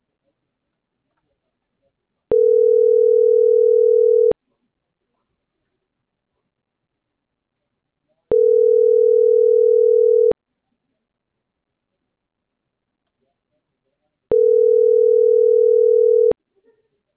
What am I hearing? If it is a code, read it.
Ringback tone (North America), 440 + 480 Hz.